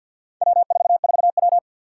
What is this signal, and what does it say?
Morse code, heard as a keyed tone, about 52 wpm, 700 Hz. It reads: W44W
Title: Four Challenging Letter-Number Combinations 50wpm